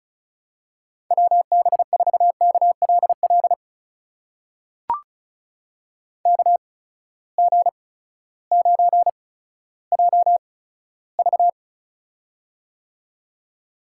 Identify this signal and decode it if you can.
Morse code, heard as a keyed tone, about 34 wpm, spaced as for 16 wpm, 700 Hz. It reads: WB4KLL KG9JV
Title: USA Call Signs - 8x Character Spacing - 35wpm